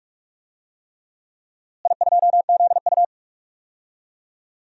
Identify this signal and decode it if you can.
Morse code, heard as a keyed tone, about 45 wpm, 700 Hz. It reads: I2ZU
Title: DXCC-100 Call Signs - 4 Characters - Repeated 3x - 45wpm